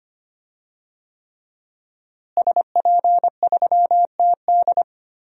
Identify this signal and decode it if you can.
Morse code, heard as a keyed tone, about 25 wpm, 700 Hz. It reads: SP3TD
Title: DXCC-100 Call Signs - 5 Characters - Repeated 3x - 25wpm